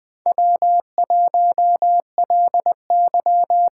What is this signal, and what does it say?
Morse code, heard as a keyed tone, about 20 wpm, 700 Hz. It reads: W1LY
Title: USA Call Signs - 5 Characters - 6x Character Spacing - 20wpm